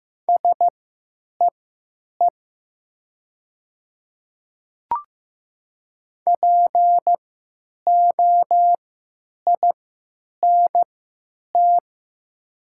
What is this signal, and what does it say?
Morse code, heard as a keyed tone, about 15 wpm, spaced as for 9 wpm, 700 Hz. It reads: SEE POINT
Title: 3x Character Spacing - Top 500 Words - 15wpm